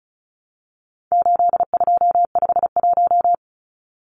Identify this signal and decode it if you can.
Morse code, heard as a keyed tone, about 34 wpm, 700 Hz. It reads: 8251
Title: Numbers - 4-Digit 35wpm